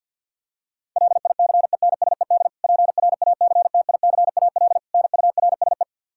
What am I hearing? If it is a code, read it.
Morse code, heard as a keyed tone, about 47 wpm, 700 Hz. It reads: LICENSED PRACTICAL NURSE
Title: Practice QSO - Exchange 3 - More Info - 50wpm